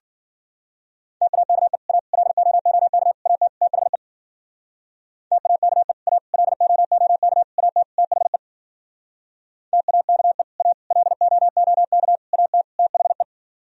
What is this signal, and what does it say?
Morse code, heard as a keyed tone, about 50 wpm, 700 Hz. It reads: TAKE A LOOK AT THE TAKE A LOOK AT THE TAKE A LOOK AT THE